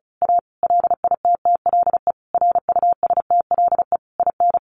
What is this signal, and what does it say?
Morse code, heard as a keyed tone, about 34 wpm, 700 Hz. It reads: A LITTLE RUSTLE IN
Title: A Sound Of Thunder 35wpm